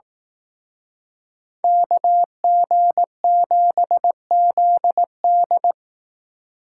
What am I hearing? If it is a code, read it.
Morse code, heard as a keyed tone, about 18 wpm, 700 Hz. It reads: KG7ZD
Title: Speed Racing - USA Call Signs 15wpm